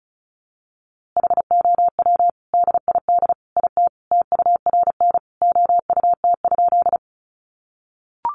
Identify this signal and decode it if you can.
Morse code, heard as a keyed tone, about 34 wpm, 700 Hz. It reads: HOW DID IT TURN OUT?
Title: A Sound Of Thunder 35wpm